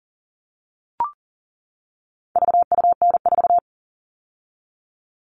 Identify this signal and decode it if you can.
Morse code, heard as a keyed tone, about 38 wpm, 700 Hz. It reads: VUN4
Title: All Cut Numbers - 4-Digit 40wpm